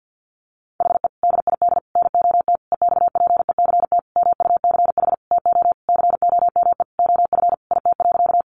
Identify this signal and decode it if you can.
Morse code, heard as a keyed tone, about 50 wpm, 700 Hz. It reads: HE DID NOT EXPECT MUCH TO COME OF IT.
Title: Sentences from Top 1100 Words 50wpm